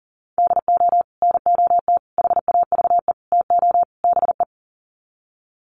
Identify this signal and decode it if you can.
Morse code, heard as a keyed tone, about 40 wpm, 700 Hz. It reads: DO NOT HAVE TO BE
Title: Sets of 5 Words 40wpm